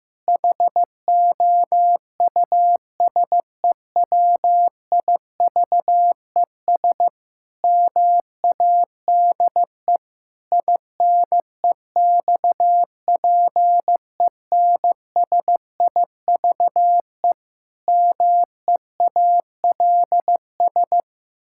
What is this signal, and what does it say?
Morse code, heard as a keyed tone, about 15 wpm, 700 Hz. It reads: HOUSEWIVES MADE INEXPENSIVE MEALS
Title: Pangram Challenge v2 - 15wpm